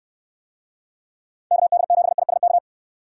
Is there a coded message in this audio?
Yes: Morse code from 1.5 to 2.6 s, heard as a keyed tone, about 64 wpm, 700 Hz. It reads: KD7ISK